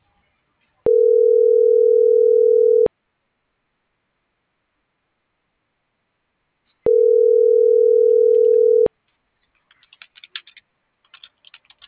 Ringback tone (North America), 440 + 480 Hz.
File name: CENTRAL SOTOVA / out alejandro.rodriguez - .54956